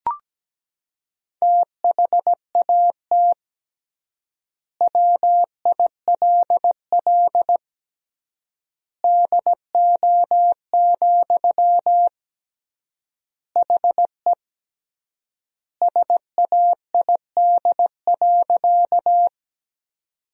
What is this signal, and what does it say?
Morse code, heard as a keyed tone, about 17 wpm, 700 Hz. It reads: THAT WILL DO, HE SAID.
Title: 3x Word-Spacing - Sentences from Top 100 Words - 17wpm